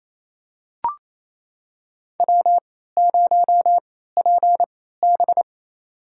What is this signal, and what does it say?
Morse code, heard as a keyed tone, about 28 wpm, spaced as for 16 wpm, 700 Hz. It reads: W0PB
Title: USA Call Signs - 3x Character Spacing - 28wpm